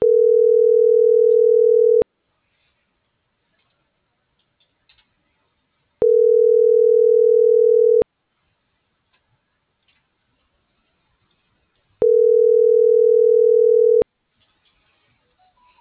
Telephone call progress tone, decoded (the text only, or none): ringback tone